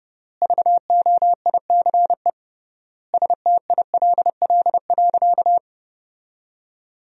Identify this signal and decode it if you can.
Morse code, heard as a keyed tone, about 31 wpm, 700 Hz. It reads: VOICE STILL.
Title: 3x Word-Spacing - Sentences from Top 500 Words - 30wpm